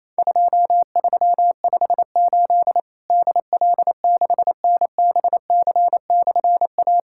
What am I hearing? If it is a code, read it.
Morse code, heard as a keyed tone, about 28 wpm, 700 Hz. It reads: 2358 DL6NBC/A
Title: Worked All Europe DX Contest - QTC Exchange - 28wpm